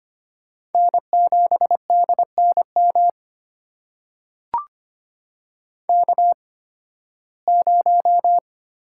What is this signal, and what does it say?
Morse code, heard as a keyed tone, about 25 wpm, spaced as for 16 wpm, 700 Hz. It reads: N7DNM K0
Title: USA Call Signs - 5 Characters - 8x Character Spacing - 25wpm